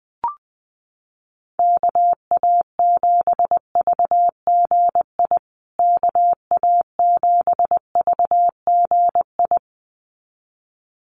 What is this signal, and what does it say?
Morse code, heard as a keyed tone, about 20 wpm, 700 Hz. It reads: KA7VGI KA7VGI
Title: USA Call Signs Repeated 2x 20wpm